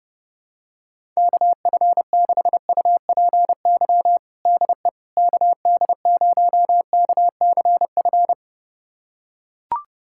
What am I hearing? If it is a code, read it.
Morse code, heard as a keyed tone, about 30 wpm, 700 Hz. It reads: KF6UPY DE KD0KCF
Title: USA Callsign DE Callsign - 6 Characters - 30wpm